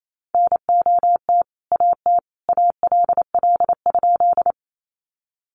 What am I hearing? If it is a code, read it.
Morse code, heard as a keyed tone, about 28 wpm, 700 Hz. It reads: NOT AT ALL?
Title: Sentences from Top 600 Words 28wpm